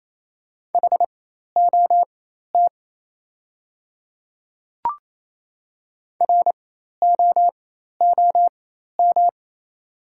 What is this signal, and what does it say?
Morse code, heard as a keyed tone, about 28 wpm, spaced as for 13 wpm, 700 Hz. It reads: HOT ROOM
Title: 4x Character Spacing - Top 300 Words - 28wpm